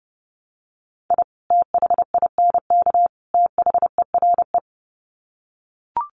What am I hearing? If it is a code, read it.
Morse code, heard as a keyed tone, about 30 wpm, 700 Hz. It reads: I THINK THERE
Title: Sets of 3 Words 30wpm